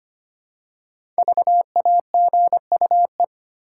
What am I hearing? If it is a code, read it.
Morse code, heard as a keyed tone, about 25 wpm, 700 Hz. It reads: VAGUE
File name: 5 Letter Words - First 3 Letters Given - 25wpm